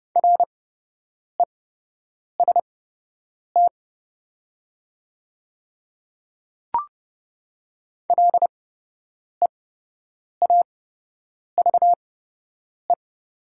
Morse code, heard as a keyed tone, about 31 wpm, spaced as for 8 wpm, 700 Hz. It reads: REST LEAVE
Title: 8x Character Spacing - Top 500 Words - 30wpm